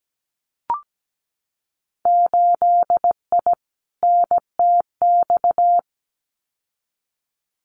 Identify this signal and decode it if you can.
Morse code, heard as a keyed tone, about 17 wpm, 700 Hz. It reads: 8I NTX
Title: Winter Field Day Exchanges Encore 17wpm